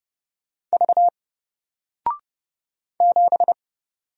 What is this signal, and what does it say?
Morse code, heard as a keyed tone, about 30 wpm, 700 Hz. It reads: V 7
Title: Single Letter-Number Alt 30wpm